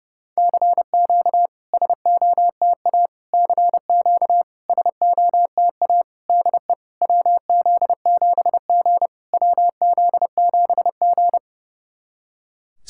Morse code, heard as a keyed tone, about 29 wpm, 690 Hz. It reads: CQ SOTA CQ SOTA DE WZ7G WZ7G
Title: SOTA QSO Practice 30wpm